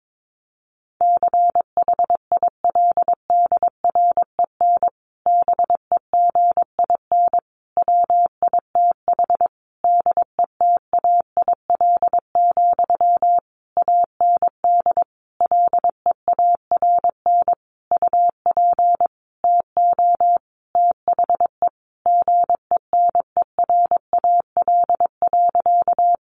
Morse code, heard as a keyed tone, about 22 wpm, 700 Hz. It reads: CHILDREN BEGIN WITH DETAIL, AND LEARN UP TO THE GENERAL.
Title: Sentences from Top 1500 Words 22wpm